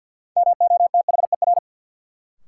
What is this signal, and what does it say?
Morse code, heard as a keyed tone, about 50 wpm, 700 Hz. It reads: MOTHER
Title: Sentences from Top 200 Words - New Word Drill - 50wpm